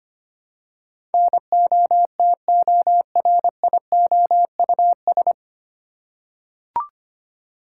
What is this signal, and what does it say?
Morse code, heard as a keyed tone, about 25 wpm, 700 Hz. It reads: NOTORIOUS
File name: Nine Letter Words - 25wpm